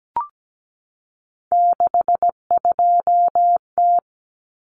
Morse code, heard as a keyed tone, about 17 wpm, 700 Hz. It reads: 62T
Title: Common Cut Numbers - 3-Digit 17wpm